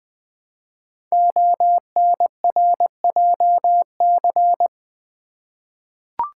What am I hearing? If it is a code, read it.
Morse code, heard as a keyed tone, about 20 wpm, 700 Hz. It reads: ONRJC
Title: Code Groups - 5 Letters - 20wpm